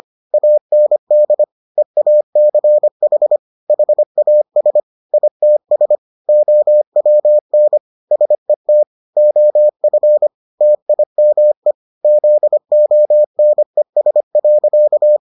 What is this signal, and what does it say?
Morse code, heard as a keyed tone, about 25 wpm, 600 Hz. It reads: AND EACH HAS ITS OWN SET OF TIME ZONES.